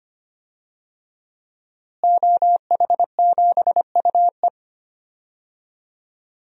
Morse code, heard as a keyed tone, about 25 wpm, 700 Hz. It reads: OH7UE
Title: DXCC-100 Call Signs - 5 Characters - Repeated 3x - 25wpm